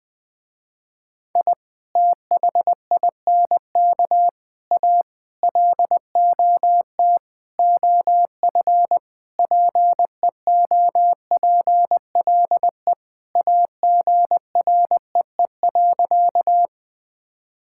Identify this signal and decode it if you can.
Morse code, heard as a keyed tone, about 20 wpm, 700 Hz. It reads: I THINK A LOT OF PEOPLE AGREE.
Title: Sentences with 5-Word Phrases - Spoken Beginning - 20wpm